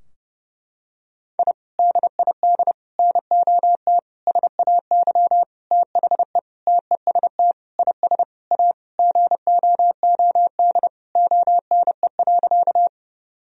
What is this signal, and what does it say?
Morse code, heard as a keyed tone, about 30 wpm, 700 Hz. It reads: I DID NOT SAY THE TEST IS A GOOD ONE.
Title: Sentences from Top 1000 Words - Review Part 3 - 30wpm